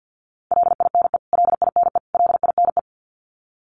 Morse code, heard as a keyed tone, about 50 wpm, 700 Hz. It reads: LINE LINE LINE
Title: Top 200 Words Repeated 50wpm